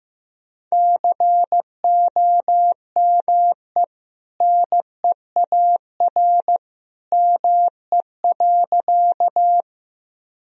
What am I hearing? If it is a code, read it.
Morse code, heard as a keyed tone, about 15 wpm, 700 Hz. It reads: COME NEAR ME.
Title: Sentences from Top 1000 Words - Review Part 1 - 15wpm